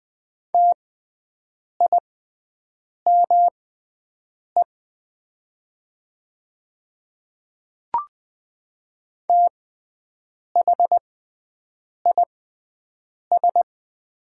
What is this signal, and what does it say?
Morse code, heard as a keyed tone, about 20 wpm, spaced as for 7 wpm, 700 Hz. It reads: TIME THIS
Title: 6x Character Spacing - Top 100 Words - 20wpm